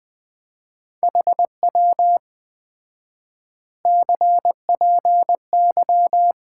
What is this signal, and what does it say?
Morse code, heard as a keyed tone, about 20 wpm, 700 Hz. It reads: HW CPY
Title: Common QSO Phrases - 4x Word-Spacing - 20wpm